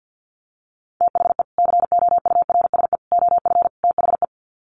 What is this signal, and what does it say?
Morse code, heard as a keyed tone, about 50 wpm, 700 Hz. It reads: THE COURSE OF THE
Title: Sets of 4 Words 50wpm